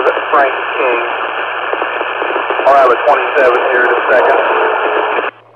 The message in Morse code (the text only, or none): none